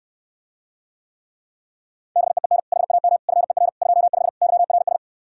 Morse code, heard as a keyed tone, about 69 wpm, 700 Hz. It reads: BEEN HAM FER 25 YRS